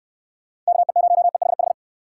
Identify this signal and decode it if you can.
Morse code, heard as a keyed tone, about 66 wpm, 700 Hz. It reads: KE0EHH